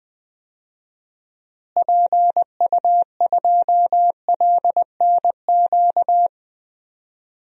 Morse code, heard as a keyed tone, about 20 wpm, 700 Hz. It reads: PU2LNQ
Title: DXCC-100 Call Signs - 6 Characters - Repeated 2x - 20wpm